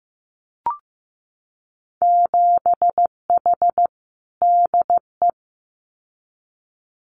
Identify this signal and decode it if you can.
Morse code, heard as a keyed tone, about 15 wpm, 700 Hz. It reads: 7H DE
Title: Winter Field Day Exchanges Encore 15wpm